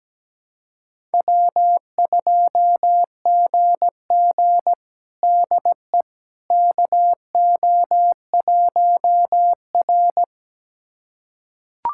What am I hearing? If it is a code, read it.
Morse code, heard as a keyed tone, about 17 wpm, 700 Hz. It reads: W2GG DE KO1R
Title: USA Callsign DE Callsign - 4 Characters - 17wpm